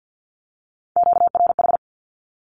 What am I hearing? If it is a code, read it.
Morse code, heard as a keyed tone, about 50 wpm, 700 Hz. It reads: QRH
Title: Q-Codes 50wpm